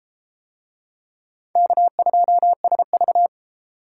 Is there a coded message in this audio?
Yes: Morse code from 1.6 to 3.3 s, heard as a keyed tone, about 32 wpm, 700 Hz. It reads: K2SV